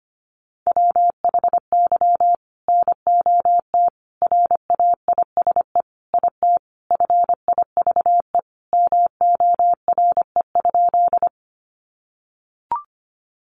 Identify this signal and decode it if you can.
Morse code, heard as a keyed tone, about 25 wpm, 700 Hz. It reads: WHY NOT RAISE IT FIVE MORE?
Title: Sentences from Top 1400 Words 25wpm